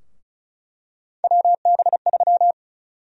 Morse code, heard as a keyed tone, about 36 wpm, 700 Hz. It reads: WB3
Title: USA Call Sign Prefixes - Repeated 2x - 35wpm